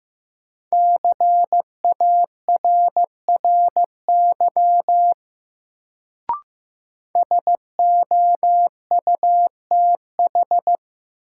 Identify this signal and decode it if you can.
Morse code, heard as a keyed tone, about 15 wpm, 700 Hz. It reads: CARRY SOUTH
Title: Sentences from Top 400 Words - New Word Drill - 15wpm